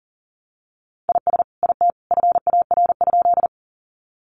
Morse code, heard as a keyed tone, about 40 wpm, 700 Hz. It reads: IS IT FAR?
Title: Sentences from Top 300 Words 40wpm